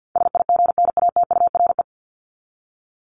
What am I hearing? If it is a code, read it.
Morse code, heard as a keyed tone, about 50 wpm, 700 Hz. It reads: SIGNATURE
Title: Top 1751-2000 Words 50wpm